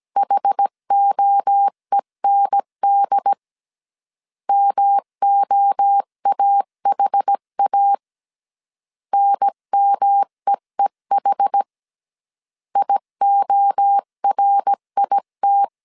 Morse code, heard as a keyed tone, about 17 wpm, 800 Hz. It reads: HOEND MOAHA NMEEH IORIT
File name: CW / DET / 1 3 1500C S A 16wpm 2141